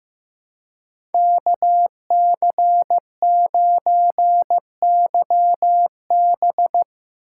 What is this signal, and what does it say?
Morse code, heard as a keyed tone, about 17 wpm, 700 Hz. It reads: KC9YB
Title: Speed Racing - USA 2x2 Call Signs 15wpm